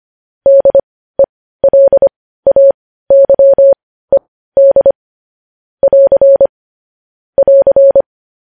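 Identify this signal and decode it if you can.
Morse code, heard as a keyed tone, about 25 wpm, spaced as for 15 wpm, 550 Hz. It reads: DELAYED <AR> <AR>
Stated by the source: Morse code news headlines sent at 15.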